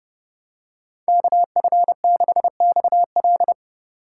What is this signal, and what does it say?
Morse code, heard as a keyed tone, about 29 wpm, 700 Hz. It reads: KF6XL